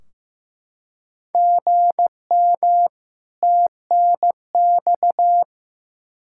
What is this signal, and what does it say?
Morse code, heard as a keyed tone, about 15 wpm, 700 Hz. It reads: GM TNX
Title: Practice QSO - Exchange 1 - Beginning - 15wpm